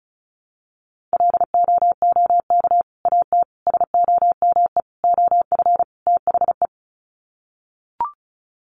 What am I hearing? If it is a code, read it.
Morse code, heard as a keyed tone, about 34 wpm, 700 Hz. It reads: LOOK AT SOME OF THE
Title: Sets of 5 Words Encore 35wpm